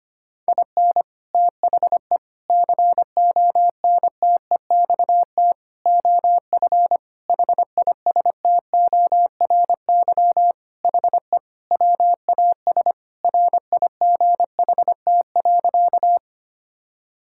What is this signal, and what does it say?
Morse code, heard as a keyed tone, about 25 wpm, 700 Hz. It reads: IN THE CONTEXT OF HISTORY HE WAS RIGHT.
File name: practice / 25wpm / Sentences with 4-Word Phrases - 25wpm